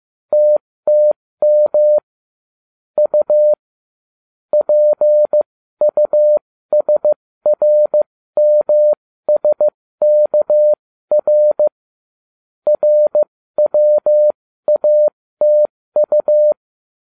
Morse code, heard as a keyed tone, about 15 wpm, spaced as for 13 wpm, 600 Hz. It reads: TTM U PUSRMSKR RWATU